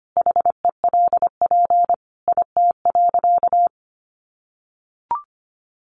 Morse code, heard as a keyed tone, about 25 wpm, 700 Hz. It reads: HELP IT.
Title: Sentences from Top 1100 Words 25wpm